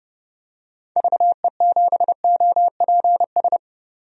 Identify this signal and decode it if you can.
Morse code, heard as a keyed tone, about 32 wpm, 700 Hz. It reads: VE7OPS
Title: DXCC-100 Call Signs - 6 Characters - 30wpm